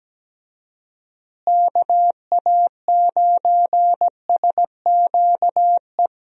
Morse code, heard as a keyed tone, about 17 wpm, 700 Hz. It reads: KA9SQE
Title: USA Call Signs - 6 Characters - Repeated 2x - 17wpm